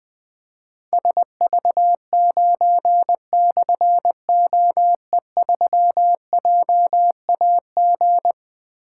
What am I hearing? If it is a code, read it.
Morse code, heard as a keyed tone, about 20 wpm, 700 Hz. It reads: SV9/OE3JAG
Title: Difficult Call Signs 20wpm